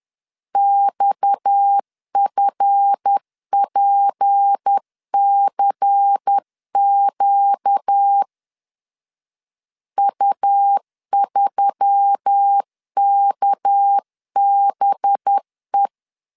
Morse code, heard as a keyed tone, about 10 wpm, 800 Hz. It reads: XFPCQ U3KBE
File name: LEZIONI / 7 GF9Z / 7 GF 10wpm 2514